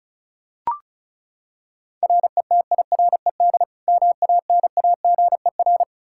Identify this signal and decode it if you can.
Morse code, heard as a keyed tone, about 34 wpm, 700 Hz. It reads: RETIRED MANAGER
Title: K1USN SST QSO Practice 35wpm